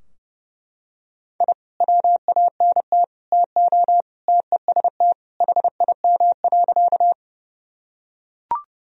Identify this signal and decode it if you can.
Morse code, heard as a keyed tone, about 32 wpm, 700 Hz. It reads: I WANT TO TEST HIM.
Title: Sentences from Top 1000 Words - Review Part 3 - 30wpm